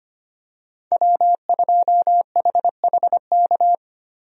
Morse code, heard as a keyed tone, about 25 wpm, 700 Hz. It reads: W2HHK
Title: USA Call Signs - 5 Characters - 25wpm